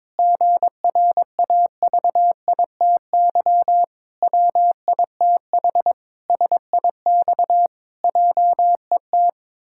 Morse code, heard as a keyed tone, about 22 wpm, 700 Hz. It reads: GRAVITY WITH SIX JET
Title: Pangram Challenge v2 - 22wpm